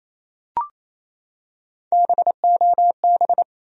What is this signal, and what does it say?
Morse code, heard as a keyed tone, about 27 wpm, 700 Hz. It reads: BOB
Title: USA Common Names - Max 4 Letters - 28wpm